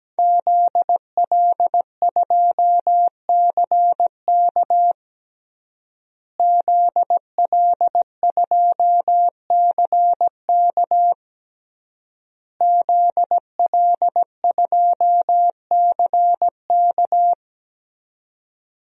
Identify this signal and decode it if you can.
Morse code, heard as a keyed tone, about 17 wpm, 700 Hz. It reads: ZL2CK ZL2CK ZL2CK
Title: DXCC-100 Call Signs - 5 Characters - Repeated 3x - 17wpm